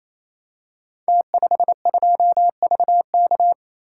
Morse code, heard as a keyed tone, about 28 wpm, 700 Hz. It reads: T52VK
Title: Code Groups - 5 Letter:Numbers - 28wpm